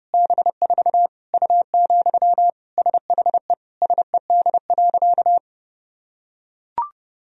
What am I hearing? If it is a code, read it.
Morse code, heard as a keyed tone, about 30 wpm, 700 Hz. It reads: B4 U, SHE SED.
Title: Sentences from Top 400 Words and Abbreviations 30wpm